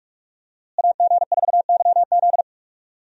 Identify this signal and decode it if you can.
Morse code, heard as a keyed tone, about 46 wpm, 700 Hz. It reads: AG4YZ
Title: USA Call Signs - 5 Characters - Repeated 2x - 45wpm